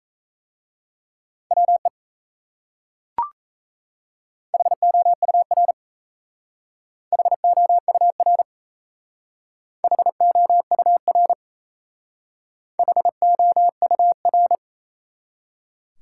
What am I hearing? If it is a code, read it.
Morse code, heard as a keyed tone, about 36 wpm, 700 Hz. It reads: WE HOUR HOUR HOUR HOUR